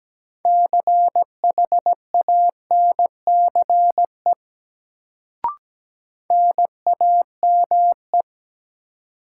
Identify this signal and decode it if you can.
Morse code, heard as a keyed tone, about 17 wpm, 700 Hz. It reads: CHANCE NAME
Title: Sets of 3 Words - New Word Drill - 17wpm